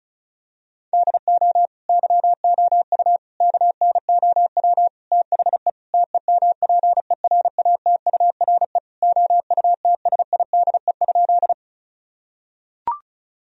Morse code, heard as a keyed tone, about 34 wpm, 700 Hz. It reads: DO YOU KNOW THE TEMPERATURE OUTSIDE?
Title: Sentences from Top 1000 Words - Review Part 4 - 35wpm